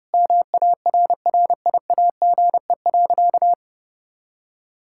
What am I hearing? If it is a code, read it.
Morse code, heard as a keyed tone, about 30 wpm, 700 Hz. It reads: MARRIAGE.
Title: Sentences from Top 1000 Words - Review Part 5 - 30wpm